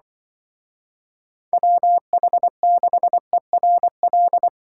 Morse code, heard as a keyed tone, about 23 wpm, 700 Hz. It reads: WH6ERL
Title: Speed Racing - USA 2x3 Call Signs 20wpm